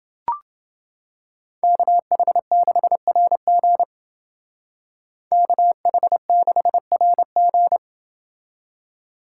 Morse code, heard as a keyed tone, about 29 wpm, 700 Hz. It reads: KH6RG KH6RG